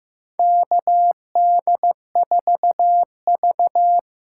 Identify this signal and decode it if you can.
Morse code, heard as a keyed tone, about 15 wpm, 700 Hz. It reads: KD4V
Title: USA Call Signs - 4 Characters - 6x Character Spacing - 15wpm